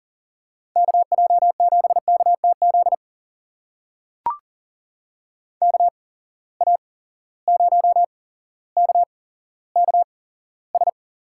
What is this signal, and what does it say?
Morse code, heard as a keyed tone, about 41 wpm, spaced as for 17 wpm, 700 Hz. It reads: KJ7KTZ KA0KKS
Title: USA Call Signs - 8x Character Spacing - 40wpm